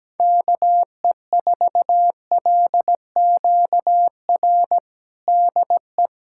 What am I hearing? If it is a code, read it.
Morse code, heard as a keyed tone, about 17 wpm, 700 Hz. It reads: KE4LQR DE
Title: USA Callsign DE Callsign - 6 Characters - 17wpm